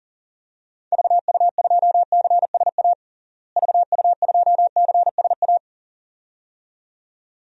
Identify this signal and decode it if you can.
Morse code, heard as a keyed tone, about 37 wpm, 700 Hz. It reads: VU2CSA VU2CSA
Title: DXCC-100 Call Signs - 6 Characters - Repeated 2x - 40wpm